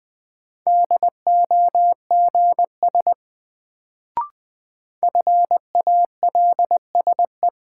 Morse code, heard as a keyed tone, about 20 wpm, 700 Hz. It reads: DOGS FALSE
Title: Sentences from Top 1600 Words - New Word Drill - 20wpm